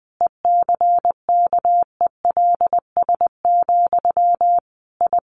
Morse code, heard as a keyed tone, about 20 wpm, 700 Hz. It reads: ECKELS, I
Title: A Sound Of Thunder 20wpm